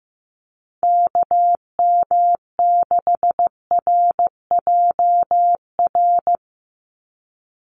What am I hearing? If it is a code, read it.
Morse code, heard as a keyed tone, about 15 wpm, 700 Hz. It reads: KM6RJR
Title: USA Call Signs 15wpm